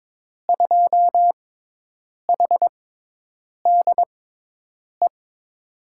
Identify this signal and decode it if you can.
Morse code, heard as a keyed tone, about 22 wpm, spaced as for 8 wpm, 700 Hz. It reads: 2HDE
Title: USA Call Signs - 6 Characters - 6x Character Spacing - 22wpm